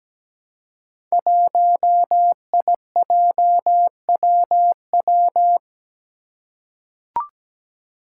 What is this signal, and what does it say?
Morse code, heard as a keyed tone, about 17 wpm, 700 Hz. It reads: 1IJWW
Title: Five Challenging Letter-Number Combinations 17wpm